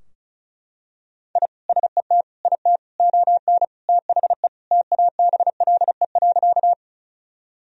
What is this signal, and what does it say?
Morse code, heard as a keyed tone, about 36 wpm, 700 Hz. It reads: I SET IT ON THE TABLE.
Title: Sentences from Top 1000 Words - Review Part 2 - 35wpm